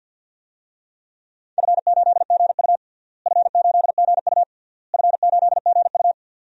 Morse code, heard as a keyed tone, about 50 wpm, 700 Hz. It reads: F8GU F8GU F8GU
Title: DXCC-100 Call Signs - 4 Characters - Repeated 3x - 50wpm